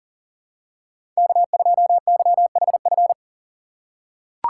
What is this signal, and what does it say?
Morse code, heard as a keyed tone, about 40 wpm, 700 Hz. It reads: K2YHF
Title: USA Call Signs - 5 Characters - 6x Character Spacing - 40wpm